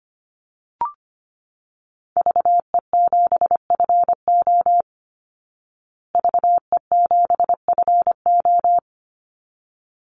Morse code, heard as a keyed tone, about 25 wpm, 700 Hz. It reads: VE7FO VE7FO
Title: Call Signs Repeated 3x - 25wpm